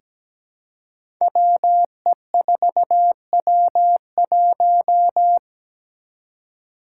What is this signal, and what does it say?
Morse code, heard as a keyed tone, about 17 wpm, 700 Hz. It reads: WE4W1
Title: Five Challenging Letter-Number Combinations 17wpm